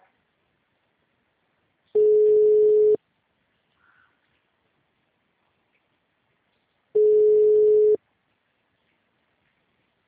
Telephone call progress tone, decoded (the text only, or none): ringback tone